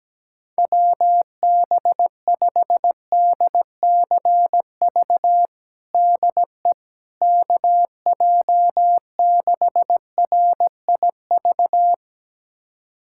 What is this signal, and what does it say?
Morse code, heard as a keyed tone, about 17 wpm, 700 Hz. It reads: WB5DCV DE KJ6RIV
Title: USA Callsign DE Callsign - 6 Characters - 17wpm